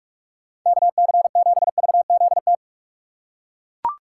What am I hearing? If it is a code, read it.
Morse code, heard as a keyed tone, about 41 wpm, 700 Hz. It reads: KC7VZT